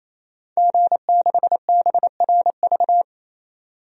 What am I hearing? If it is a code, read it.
Morse code, heard as a keyed tone, about 28 wpm, 700 Hz. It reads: G6BRV
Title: DXCC-100 Call Signs - 5 Characters - 28wpm